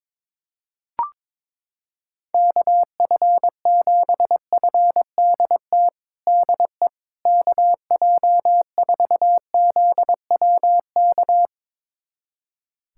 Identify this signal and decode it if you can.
Morse code, heard as a keyed tone, about 22 wpm, 700 Hz. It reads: KF7FDT DE KJ4ZWK
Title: USA Callsign DE Callsign - 6 Characters - 22wpm